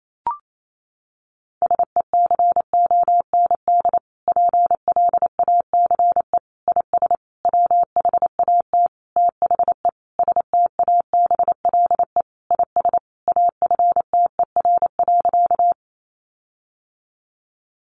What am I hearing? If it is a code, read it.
Morse code, heard as a keyed tone, about 28 wpm, 700 Hz. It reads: SECOND PLACE IS WHAT THE STABLE IS AFTER.
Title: Sentences from Top 1500 Words 28wpm